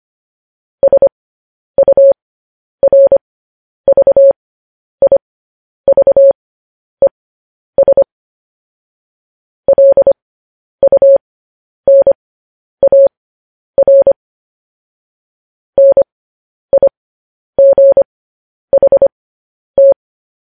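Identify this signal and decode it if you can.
Morse code, heard as a keyed tone, about 25 wpm, spaced as for 10 wpm, 550 Hz. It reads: SURVIVES LUNAR NIGHT
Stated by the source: Morse code news headlines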